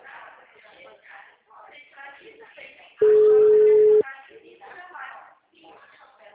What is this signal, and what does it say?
Ringback tone (Europe, 425 Hz), with speech.
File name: out- -2031- - - .48372